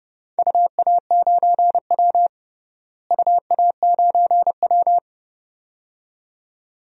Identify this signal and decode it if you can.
Morse code, heard as a keyed tone, about 30 wpm, 700 Hz. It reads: UA9W UA9W
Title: DXCC-100 Call Signs - 4 Characters - Repeated 2x - 30wpm